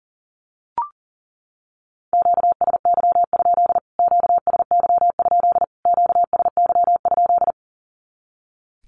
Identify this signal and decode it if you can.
Morse code, heard as a keyed tone, about 40 wpm, 700 Hz. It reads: QSY? QSY? QSY?
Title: Q-Codes Repeated 3x 40wpm